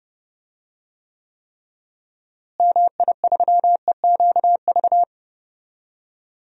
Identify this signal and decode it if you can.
Morse code, heard as a keyed tone, about 30 wpm, 700 Hz. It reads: MI3EQV
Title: DXCC-100 Call Signs - 6 Characters - Repeated 3x - 30wpm